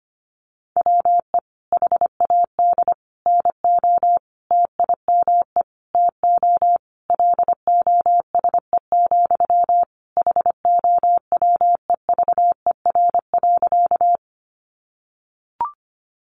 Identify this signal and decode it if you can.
Morse code, heard as a keyed tone, about 25 wpm, 690 Hz. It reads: WE HAD NO TIME TO LOSE, HOWEVER.
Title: Sentences from Top 1100 Words 25wpm